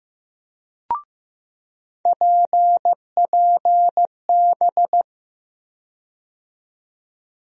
Morse code, heard as a keyed tone, about 15 wpm, 700 Hz. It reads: PPB
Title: USA Call Sign Postfixes - 3 Letters 15wpm